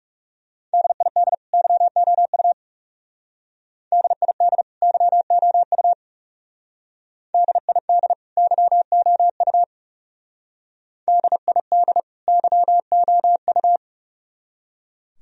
Morse code, heard as a keyed tone, about 38 wpm, 700 Hz. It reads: DID YOU DID YOU DID YOU DID YOU